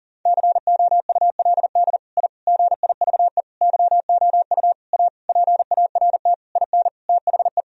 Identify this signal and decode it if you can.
Morse code, heard as a keyed tone, about 37 wpm, 700 Hz. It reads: COULD I GIVE YOU A PART IN THE
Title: Sentences from Top 1000 Words - Review Part 1 - 40wpm